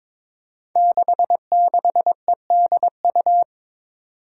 Morse code, heard as a keyed tone, about 22 wpm, 700 Hz. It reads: 66EDU
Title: Five Challenging Letter-Number Combinations 22wpm